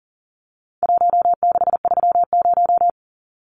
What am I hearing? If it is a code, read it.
Morse code, heard as a keyed tone, about 40 wpm, 700 Hz. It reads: 1630
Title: Numbers - Contest Style - 4-Digit 40wpm